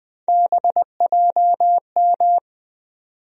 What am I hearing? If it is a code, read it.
Morse code, heard as a keyed tone, about 20 wpm, 700 Hz. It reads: BJM
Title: USA Call Sign Postfixes - 3 Letters 20wpm